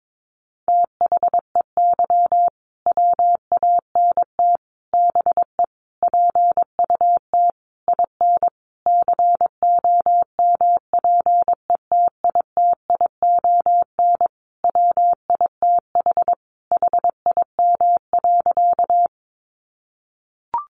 Morse code, heard as a keyed tone, about 22 wpm, 700 Hz. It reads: THEY WANT BE PUT IN COMPETITION WITH HIM.
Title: Sentences from Top 1000 Words 22wpm